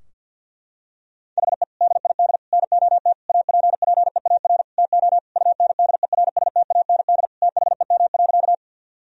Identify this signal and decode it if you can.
Morse code, heard as a keyed tone, about 50 wpm, 700 Hz. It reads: HE DID NOT APPEAR TO UNDERSTAND THEM.
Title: Sentences from Top 1000 Words - Review Part 2 - 50wpm